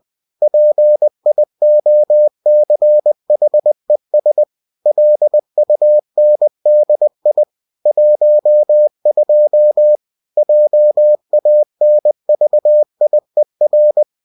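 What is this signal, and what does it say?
Morse code, heard as a keyed tone, about 20 wpm, 600 Hz. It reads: PIOCHES LUNDI 12 JANVIER
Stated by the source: Bulletin sent in CW.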